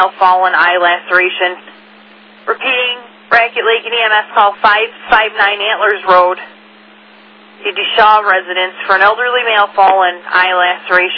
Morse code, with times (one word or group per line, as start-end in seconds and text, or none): none